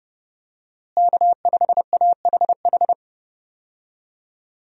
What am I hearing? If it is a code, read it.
Morse code, heard as a keyed tone, about 30 wpm, 700 Hz. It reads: K5AHH
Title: USA Call Signs - 5 Characters - Repeated 3x - 30wpm